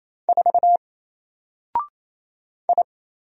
Morse code, heard as a keyed tone, about 27 wpm, 700 Hz. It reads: 4 I
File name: Single-Letter-Number-28wpm